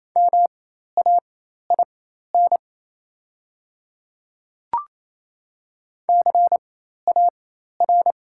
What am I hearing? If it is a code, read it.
Morse code, heard as a keyed tone, about 28 wpm, spaced as for 13 wpm, 700 Hz. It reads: MAIN CAR
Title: 4x Character Spacing - Top 300 Words - 28wpm